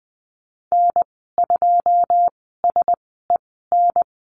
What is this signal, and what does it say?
Morse code, heard as a keyed tone, about 20 wpm, spaced as for 15 wpm, 700 Hz. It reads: N2SEN
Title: 088 Lesson 15g - Number 5 - Call Signs 20wpm